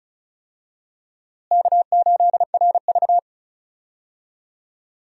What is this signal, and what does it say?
Morse code, heard as a keyed tone, about 36 wpm, 700 Hz. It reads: K8RV
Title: USA Call Signs - 4 Characters - Repeated 2x - 35wpm